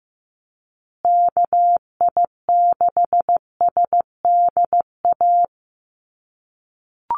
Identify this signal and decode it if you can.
Morse code, heard as a keyed tone, about 15 wpm, 700 Hz. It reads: KI6SDA
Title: USA Call Signs Repeated 3x 15wpm